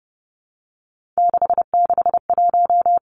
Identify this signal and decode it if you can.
Morse code, heard as a keyed tone, about 30 wpm, 700 Hz. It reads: 661
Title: Three Challenging Letter-Number Combinations 30wpm